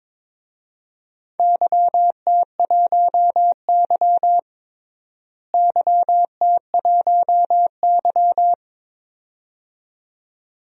Morse code, heard as a keyed tone, about 22 wpm, 700 Hz. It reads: YT1Y YT1Y
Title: DXCC-100 Call Signs - 4 Characters - Repeated 2x - 22wpm